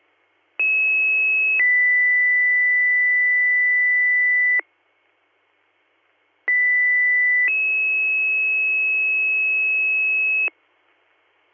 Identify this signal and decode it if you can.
Two-tone page, heard as a one-second tone, then a three-second tone, about 2600 then 2000 Hz, about 2000 then 2600 Hz. It is heard from 0.6 to 10.5 s.